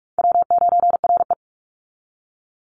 Morse code, heard as a keyed tone, about 44 wpm, 700 Hz. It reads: W9RE
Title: Easy Call Signs - 45wpm